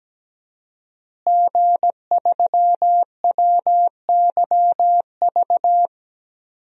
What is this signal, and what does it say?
Morse code, heard as a keyed tone, about 17 wpm, 700 Hz. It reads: G3WYV
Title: DXCC-100 Call Signs - 5 Characters - Repeated 3x - 17wpm